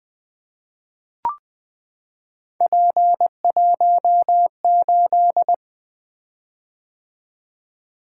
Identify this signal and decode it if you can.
Morse code, heard as a keyed tone, about 21 wpm, 700 Hz. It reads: P18
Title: Three Letter-Number Combinations - 20wpm